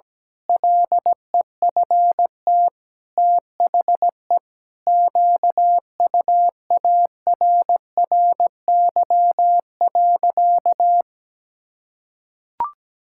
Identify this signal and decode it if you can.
Morse code, heard as a keyed tone, about 17 wpm, 700 Hz. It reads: LEFT THE QUARRY.
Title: Pangram Challenge v2 - 17wpm